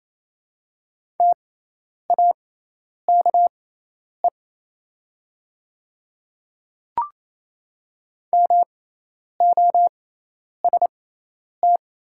Morse code, heard as a keyed tone, about 28 wpm, spaced as for 10 wpm, 700 Hz. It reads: TAKE MOST
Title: 6x Character Spacing - Top 300 Words - 28wpm